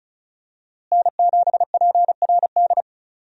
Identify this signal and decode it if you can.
Morse code, heard as a keyed tone, about 36 wpm, 700 Hz. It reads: N7PRD
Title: USA Call Signs - 5 Characters - Repeated 2x - 35wpm